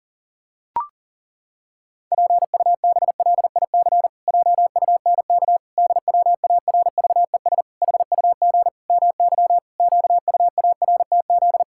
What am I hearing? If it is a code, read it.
Morse code, heard as a keyed tone, about 37 wpm, 700 Hz. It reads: PUBLIC JUNK DWARVES HUG MY QUARTZ
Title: Pangram Challenge v2 - 40wpm